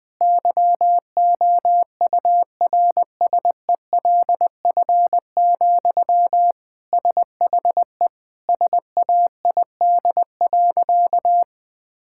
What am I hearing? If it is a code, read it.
Morse code, heard as a keyed tone, about 21 wpm, 700 Hz. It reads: YOURSELF, SHE SAID.
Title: Sentences from Top 1600 Words 20wpm